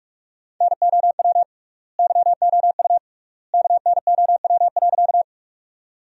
Morse code, heard as a keyed tone, about 43 wpm, 700 Hz. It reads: NOW YOU KNOW.
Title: 3x Word-Spacing - Sentences from Top 100 Words - 45wpm